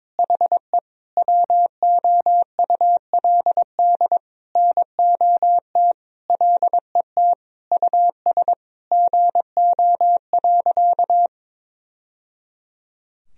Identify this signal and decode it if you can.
Morse code, heard as a keyed tone, about 22 wpm, 700 Hz. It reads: HE WOULD NOT LET US GO.
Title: Sentences from Top 1000 Words - Review Part 2 - 22wpm